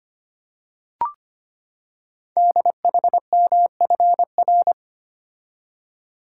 Morse code, heard as a keyed tone, about 25 wpm, 700 Hz. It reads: DHMFR
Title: Code Groups - 5 Letters - 25wpm